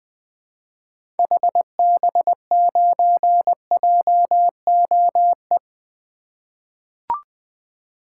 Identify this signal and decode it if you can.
Morse code, heard as a keyed tone, about 20 wpm, 700 Hz. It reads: HB9JOE
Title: 2023 SOTA Activator Call Signs - 20wpm